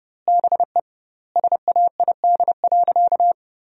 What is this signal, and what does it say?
Morse code, heard as a keyed tone, about 30 wpm, 700 Hz. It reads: BE SAID.
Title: 2x Word-Spacing - Sentences from Top 100 Words - 30wpm